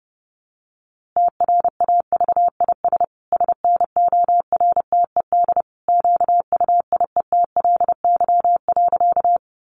Morse code, heard as a keyed tone, about 28 wpm, 700 Hz. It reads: TRAVIS SNORTED QUIETLY.
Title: A Sound Of Thunder 30wpm